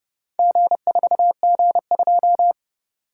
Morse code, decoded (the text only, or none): G4G2